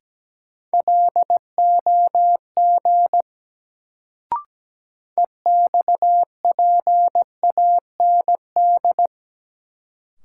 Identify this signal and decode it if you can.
Morse code, heard as a keyed tone, about 17 wpm, 700 Hz. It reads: LOG EXPAND
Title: Sentences from Top 1400 Words - New Word Drill Alt - 17wpm